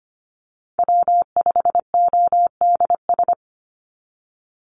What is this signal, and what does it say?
Morse code, heard as a keyed tone, about 25 wpm, 700 Hz. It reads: W5ODS
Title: NA SOTA Activator Callsigns 25wpm